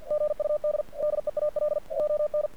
Morse code, heard as a keyed tone, about 48 wpm, 600 Hz. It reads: GUD DEAL ON
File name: archived / CRSnet